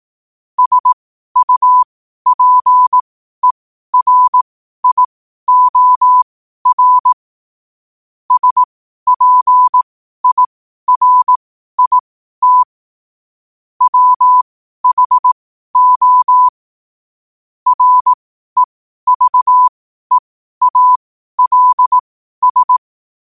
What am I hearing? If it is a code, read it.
Morse code, heard as a keyed tone, about 18 wpm, spaced as for 12 wpm, 1000 Hz. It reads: SUPERIOR SPIRIT WHO REVEALS
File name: 12wpm